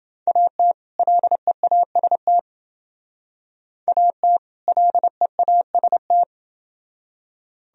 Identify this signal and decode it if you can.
Morse code, heard as a keyed tone, about 29 wpm, 700 Hz. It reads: AT LEAST AT LEAST